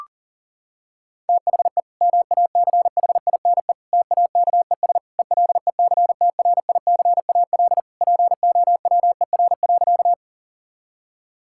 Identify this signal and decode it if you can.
Morse code, heard as a keyed tone, about 40 wpm, 700 Hz. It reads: THE MACHINE TAKES ELECTRICAL POWER.
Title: Sentences from Top 1000 Words - Review Part 6 - 40wpm